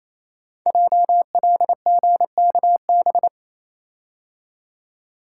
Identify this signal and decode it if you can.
Morse code, heard as a keyed tone, about 28 wpm, 700 Hz. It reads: JLGKB
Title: Code Groups - 5 Letters - 28wpm